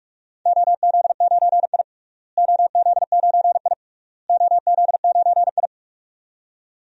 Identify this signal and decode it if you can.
Morse code, heard as a keyed tone, about 41 wpm, 700 Hz. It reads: OZ9I OZ9I OZ9I
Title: DXCC-100 Call Signs - 4 Characters - Repeated 3x - 45wpm